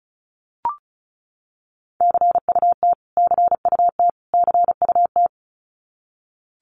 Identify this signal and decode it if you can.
Morse code, heard as a keyed tone, about 34 wpm, 700 Hz. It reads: CUT CUT CUT
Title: Top 500 Words Repeated 35wpm